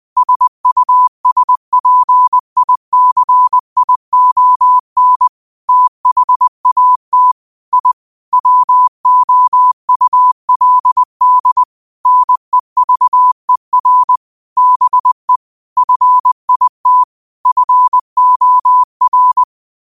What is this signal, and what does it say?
Morse code, heard as a keyed tone, about 20 wpm, 1000 Hz. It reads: SUSPICION THAT I WOULD NEVER BE FIT FOR
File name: qotdfiles / 20wpm